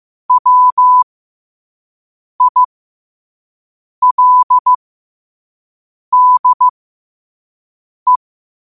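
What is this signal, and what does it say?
Morse code, heard as a keyed tone, about 15 wpm, spaced as for 5 wpm, 1000 Hz. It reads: WILDE
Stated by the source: Morse code quote of the day at 5 wpm.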